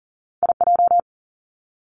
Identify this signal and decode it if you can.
Morse code, heard as a keyed tone, about 40 wpm, 700 Hz. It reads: IJ
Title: Two Challenging Letter-Number Combinations 40wpm